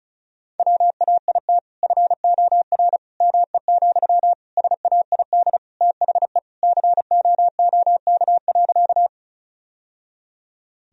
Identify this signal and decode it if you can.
Morse code, heard as a keyed tone, about 34 wpm, 700 Hz. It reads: WAIT FOR ME, SAID THE COOK.
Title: Sentences from Top 1000 Words - Review Part 4 - 35wpm